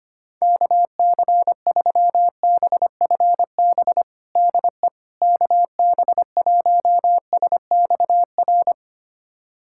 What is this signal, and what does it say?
Morse code, heard as a keyed tone, about 25 wpm, 700 Hz. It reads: KC3BFB DE KB1SXR
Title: USA Callsign DE Callsign - 6 Characters - 25wpm